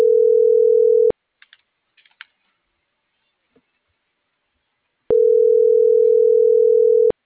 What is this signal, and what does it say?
Ringback tone (North America), 440 + 480 Hz.
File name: CENTRAL NMP / out melanie.diaznmp - .40379